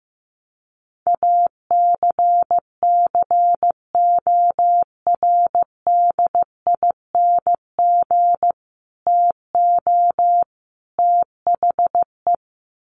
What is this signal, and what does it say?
Morse code, heard as a keyed tone, about 15 wpm, 700 Hz. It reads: ACCORDING TO THE
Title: Sets of 3 Words 15wpm